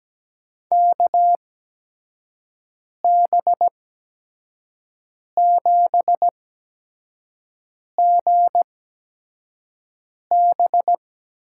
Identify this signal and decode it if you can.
Morse code, heard as a keyed tone, about 17 wpm, spaced as for 5 wpm, 700 Hz. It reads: KB7GB
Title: USA Call Signs - 6 Characters - 8x Character Spacing - 17wpm